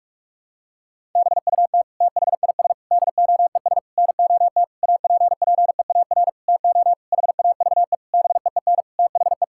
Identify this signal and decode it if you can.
Morse code, heard as a keyed tone, about 46 wpm, 700 Hz. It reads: BUT THIS DOES NOT APPEAR TO HAVE BEEN THE
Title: Sentences from Top 1000 Words - Review Part 6 - 45wpm